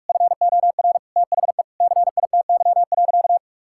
Morse code, heard as a keyed tone, about 46 wpm, 700 Hz. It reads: FOR THE CITY.
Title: Sentences from Top 1600 Words 45wpm